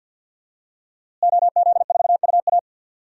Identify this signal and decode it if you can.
Morse code, heard as a keyed tone, about 52 wpm, 700 Hz. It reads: OZ4UA
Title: DXCC-100 Call Signs - 5 Characters - Repeated 2x - 50wpm